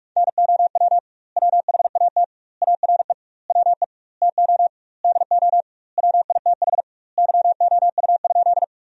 Morse code, heard as a keyed tone, about 46 wpm, 700 Hz. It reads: NOW WHAT ARE WE TO DO WITH YOU?
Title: 2x Word-Spacing - Sentences from Top 100 Words - 45wpm